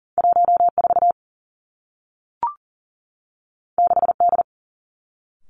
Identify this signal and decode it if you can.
Morse code, heard as a keyed tone, about 38 wpm, 700 Hz. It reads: 1 4 6 D
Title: Two Challenging Letter-Number Combinations 40wpm